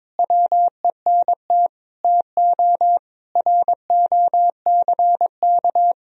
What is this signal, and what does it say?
Morse code, heard as a keyed tone, about 22 wpm, 700 Hz. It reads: WENT TO ROCK
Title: Sentences from Top 1000 Words - Review Part 3 - 22wpm